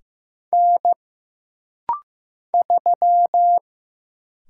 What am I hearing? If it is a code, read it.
Morse code, heard as a keyed tone, about 15 wpm, 700 Hz. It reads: N 3
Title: Single Letter-Number Alt - Rapid-Fire 15wpm